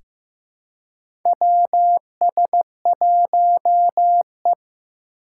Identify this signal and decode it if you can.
Morse code, heard as a keyed tone, about 15 wpm, 700 Hz. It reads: WS1E
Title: USA Call Signs - 4 Characters - 4x Character Spacing - 15wpm